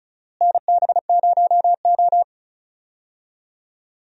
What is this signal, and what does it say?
Morse code, heard as a keyed tone, about 36 wpm, 700 Hz. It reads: NB0O
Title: USA Call Signs - 4 Characters - Repeated 2x - 35wpm